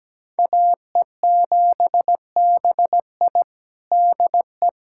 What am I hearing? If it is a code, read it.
Morse code, heard as a keyed tone, about 17 wpm, 700 Hz. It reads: AE7BI DE
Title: USA Callsign DE Callsign 17wpm